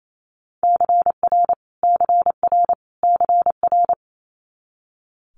Morse code, heard as a keyed tone, about 28 wpm, 700 Hz. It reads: CR CR CR
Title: Multi-letter Phonemes Repeated 3x 28wpm